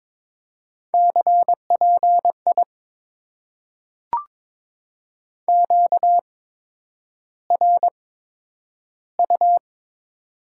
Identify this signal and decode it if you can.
Morse code, heard as a keyed tone, about 22 wpm, spaced as for 9 wpm, 700 Hz. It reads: CPI QRU
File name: Top 90 QSO Elements - 8x Character Spacing - 22wpm